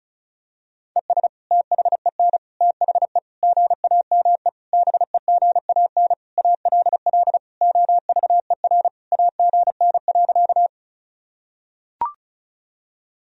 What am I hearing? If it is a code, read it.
Morse code, heard as a keyed tone, about 34 wpm, 700 Hz. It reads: ES THEN THE GAME BEGAN ALL OVER AGN.
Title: Sentences from Top 500 Words and Abbreviations 35wpm